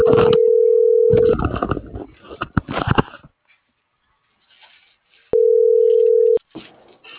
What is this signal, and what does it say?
Ringback tone (North America), 440 + 480 Hz.